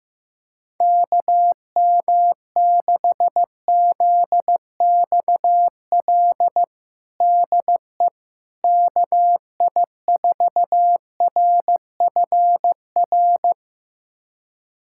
Morse code, heard as a keyed tone, about 15 wpm, 700 Hz. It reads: KM6ZXL DE KI4RFR
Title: USA Callsign DE Callsign 15wpm